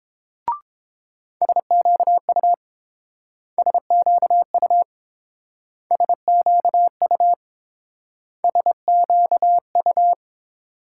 Morse code, heard as a keyed tone, about 29 wpm, 700 Hz. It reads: SQU SQU SQU SQU